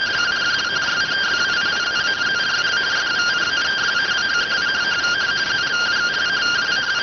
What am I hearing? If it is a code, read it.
Two-tone teleprinter signal, about 170 Hz shift.